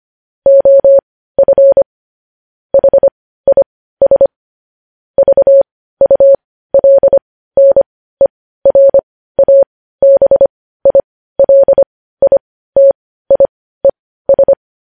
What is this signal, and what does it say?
Morse code, heard as a keyed tone, about 25 wpm, spaced as for 15 wpm, 550 Hz. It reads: OF HIS VULNERABILITIES
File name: podcast / News headlines-1x-25x15